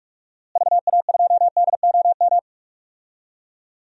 Morse code, heard as a keyed tone, about 46 wpm, 700 Hz. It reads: VA2DOM
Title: DXCC-100 Call Signs - 6 Characters - Repeated 3x - 45wpm